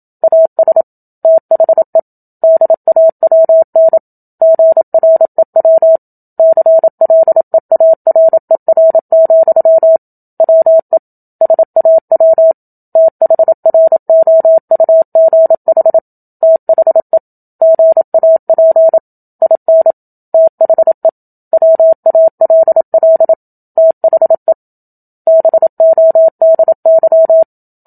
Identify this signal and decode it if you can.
Morse code, heard as a keyed tone, about 27 wpm, 650 Hz. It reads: AS THE DAWN GREW CLEARER, WE SAW THROUGH THE GAP IN THE WALL THE BODY
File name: Book Two / 01-Chapter 1 - 27 WPM